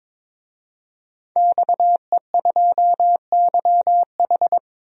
Morse code, heard as a keyed tone, about 22 wpm, 700 Hz. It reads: XE2YH
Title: DXCC-100 Call Signs - 5 Characters - 22wpm